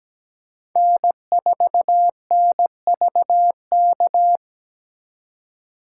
Morse code, heard as a keyed tone, about 20 wpm, 700 Hz. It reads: N4NVK
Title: Speed Racing - USA 1x3 Call Signs 17wpm